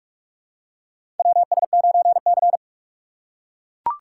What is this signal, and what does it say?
Morse code, heard as a keyed tone, about 42 wpm, 700 Hz. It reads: WS9C
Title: USA Call Signs - 4 Characters - 4x Character Spacing - 45wpm